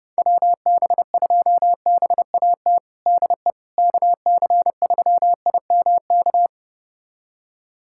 Morse code, heard as a keyed tone, about 30 wpm, 700 Hz. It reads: WB2BAT DE KC3IMK
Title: USA Callsign DE Callsign - 6 Characters - 30wpm